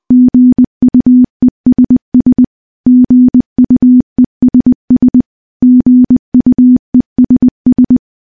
Morse code, heard as a keyed tone, about 20 wpm, 250 Hz. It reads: GUESS GUESS GUESS
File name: morse-training-20WPM